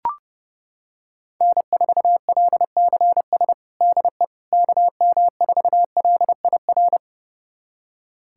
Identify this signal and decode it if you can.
Morse code, heard as a keyed tone, about 32 wpm, 700 Hz. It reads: N4LCS DE KM4LIR
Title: USA Callsign DE Callsign 30wpm